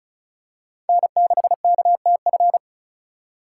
Morse code, heard as a keyed tone, about 34 wpm, 700 Hz. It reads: N6KTF
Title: USA Call Signs - 5 Characters - 3x Character Spacing - 35wpm